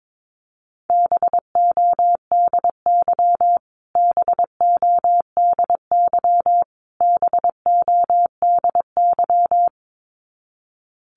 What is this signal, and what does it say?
Morse code, heard as a keyed tone, about 22 wpm, 700 Hz. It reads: BODY BODY BODY
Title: Top 500 Words Repeated 22wpm